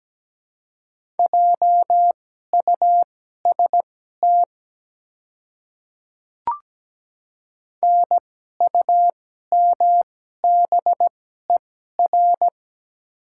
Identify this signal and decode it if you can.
Morse code, heard as a keyed tone, about 17 wpm, spaced as for 12 wpm, 700 Hz. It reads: JUST NUMBER
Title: 2x Character Spacing - Top 200 Words - 17wpm